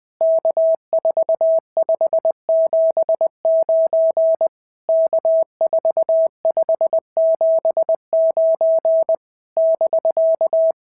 Morse code, heard as a keyed tone, about 18 wpm, 650 Hz. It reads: K4579 K4579 <BK>
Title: POTA QSO Practice - 20wpm